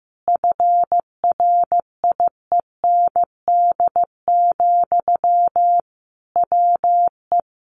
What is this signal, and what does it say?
Morse code, heard as a keyed tone, about 15 wpm, 700 Hz. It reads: FRIEND, WE
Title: Sentences from Top 1300 Words 15wpm